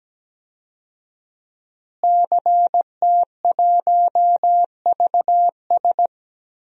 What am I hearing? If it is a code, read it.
Morse code, heard as a keyed tone, about 17 wpm, 700 Hz. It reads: CT1VS
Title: DXCC-100 Call Signs - 5 Characters - Repeated 2x - 17wpm